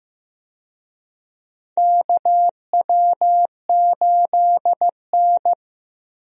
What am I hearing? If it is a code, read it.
Morse code, heard as a keyed tone, about 15 wpm, 700 Hz. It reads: KW8N
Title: USA Call Signs - 4 Characters - Repeated 2x - 15wpm